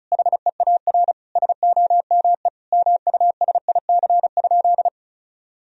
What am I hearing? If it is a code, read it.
Morse code, heard as a keyed tone, about 36 wpm, 700 Hz. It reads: HEAR SOME MUSIC?
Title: Sentences from Top 1000 Words - Review Part 1 - 35wpm